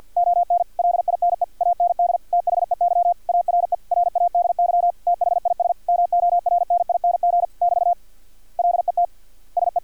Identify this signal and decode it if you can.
Morse code, heard as a keyed tone, about 50 wpm, 700 Hz. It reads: ON LINE AND THEY ARE RADY THIS MORNING <BT> LET SE
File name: archived / CRSnet